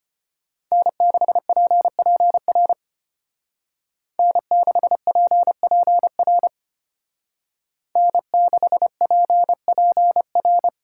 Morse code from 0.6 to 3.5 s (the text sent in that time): N6PPR